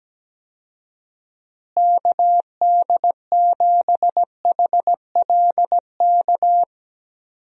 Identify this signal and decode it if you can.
Morse code, heard as a keyed tone, about 17 wpm, 700 Hz. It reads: KD7HLK
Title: USA Call Signs - 6 Characters - Repeated 2x - 17wpm